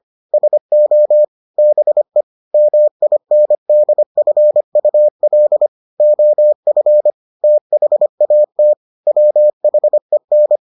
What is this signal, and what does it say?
Morse code, heard as a keyed tone, about 25 wpm, 600 Hz. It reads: SO BE MINDFUL OF THAT WHEN